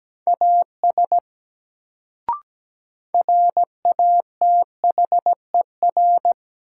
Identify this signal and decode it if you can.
Morse code, heard as a keyed tone, about 17 wpm, 700 Hz. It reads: AS RATHER
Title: Sets of 2 Words - New Word Drill - 17wpm